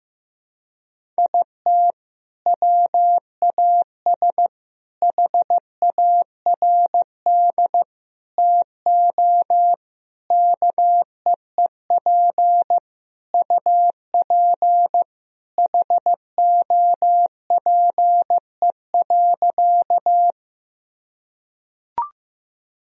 Morse code, heard as a keyed tone, about 15 wpm, 700 Hz. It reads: IT WAS HARD TO KEEP UP HOPE.
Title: Sentences from Top 1000 Words - Review Part 1 - 15wpm